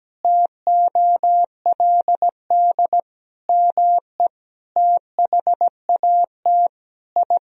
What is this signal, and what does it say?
Morse code, heard as a keyed tone, about 17 wpm, 700 Hz. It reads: TOLD ME THAT I
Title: Sentences from Top 1000 Words - Review Part 1 - 17wpm